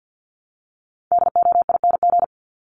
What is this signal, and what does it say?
Morse code, heard as a keyed tone, about 50 wpm, 700 Hz. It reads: DOING
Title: ING Suffix - Top 100 Words with Base Words 50wpm